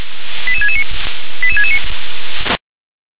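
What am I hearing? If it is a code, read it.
Five-tone selective call (ZVEI), heard as quick runs of five tones, tones of about 70 ms. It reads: 88669 88669
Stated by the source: ZVEI five-tone selective call 88713 received on an SDR.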